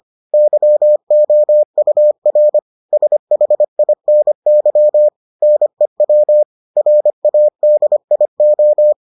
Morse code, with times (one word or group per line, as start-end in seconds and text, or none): 0.3-2.6 YOUR
2.9-5.1 SHINY
5.4-6.4 NEW
6.8-8.9 RADIO